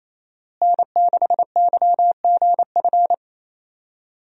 Morse code, heard as a keyed tone, about 28 wpm, 700 Hz. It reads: N6YGF
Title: USA Call Signs - 5 Characters - Repeated 3x - 28wpm